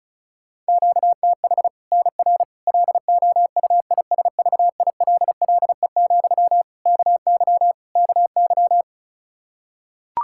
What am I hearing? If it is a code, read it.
Morse code, heard as a keyed tone, about 34 wpm, 700 Hz. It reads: QTH NR LOUISVILLE, KY KY
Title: K1USN SST QSO Practice 35wpm